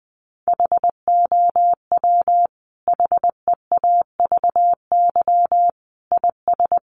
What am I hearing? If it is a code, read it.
Morse code, heard as a keyed tone, about 20 wpm, 700 Hz. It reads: HOW HEAVY IS
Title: Sentences from Top 1100 Words 20wpm